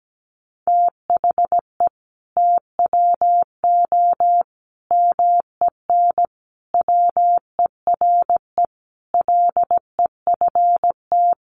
Morse code, heard as a keyed tone, about 17 wpm, 700 Hz. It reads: THE TWO MEN WERE LEFT
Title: Sentences from Top 1000 Words 17wpm